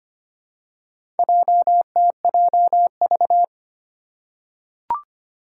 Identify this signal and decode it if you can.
Morse code, heard as a keyed tone, about 25 wpm, 700 Hz. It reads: JTJV
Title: Four Challenging Letter-Number Combinations 25wpm